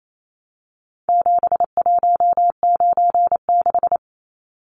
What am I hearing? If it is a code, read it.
Morse code, heard as a keyed tone, about 28 wpm, 700 Hz. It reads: 7196
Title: Numbers - 4-Digit 28wpm